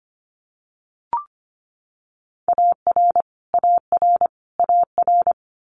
Morse code, heard as a keyed tone, about 25 wpm, 700 Hz. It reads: AR AR AR
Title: Multi-letter Phonemes Repeated 3x 25wpm